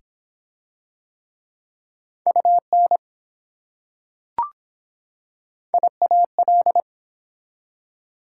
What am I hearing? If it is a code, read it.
Morse code, heard as a keyed tone, about 23 wpm, 700 Hz. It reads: UN IAL